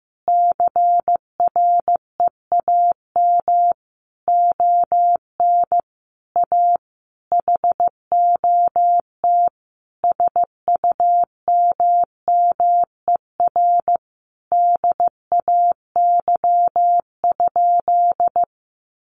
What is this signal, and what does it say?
Morse code, heard as a keyed tone, about 15 wpm, 700 Hz. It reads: CREAM ON A HOT SUMMER DAY?
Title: Sentences from Top 1500 Words 15wpm